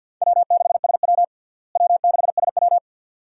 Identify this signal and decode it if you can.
Morse code, heard as a keyed tone, about 47 wpm, 700 Hz. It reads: W6SW W6SW
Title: USA Call Signs - 4 Characters - Repeated 2x - 50wpm